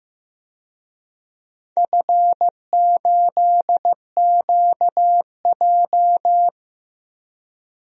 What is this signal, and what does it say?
Morse code, heard as a keyed tone, about 15 wpm, 700 Hz. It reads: F8QJ
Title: DXCC-100 Call Signs - 4 Characters - 15wpm